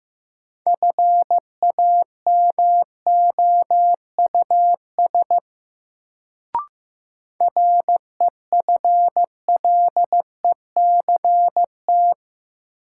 Morse code, heard as a keyed tone, about 15 wpm, 700 Hz. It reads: FAMOUS REFLECT
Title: Sentences from Top 1500 Words - New Word Drill - 15wpm